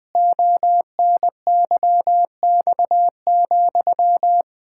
Morse code, heard as a keyed tone, about 20 wpm, 700 Hz. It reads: ONYX,
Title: Pangram Challenge v2 - 20wpm